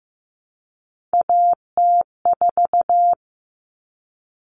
Morse code, heard as a keyed tone, about 15 wpm, 700 Hz. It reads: AT4
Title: Common Cut Numbers - 3-Digit 15wpm